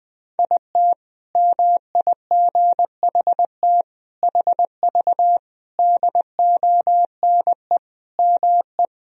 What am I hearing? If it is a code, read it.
Morse code, heard as a keyed tone, about 20 wpm, 700 Hz. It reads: IT MIGHT HV DONE ME
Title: Sentences from Top 500 Words and Abbreviations 20wpm